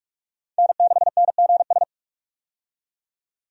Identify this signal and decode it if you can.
Morse code, heard as a keyed tone, about 41 wpm, 700 Hz. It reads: N6NGS
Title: USA Call Signs - 5 Characters - 45wpm